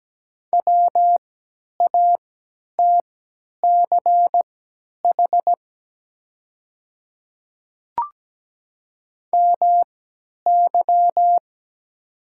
Morse code, heard as a keyed tone, about 17 wpm, spaced as for 10 wpm, 700 Hz. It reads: WATCH MY
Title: 3x Character Spacing - Top 300 Words - 17wpm